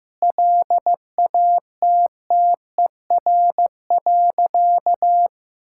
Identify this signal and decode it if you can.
Morse code, heard as a keyed tone, about 15 wpm, 700 Hz. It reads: LATTER.
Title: Sentences from Top 1600 Words 15wpm